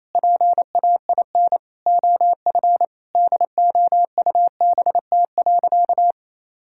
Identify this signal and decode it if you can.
Morse code, heard as a keyed tone, about 28 wpm, 700 Hz. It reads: PAIN OF DOUBT.
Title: Sentences from Top 1600 Words 28wpm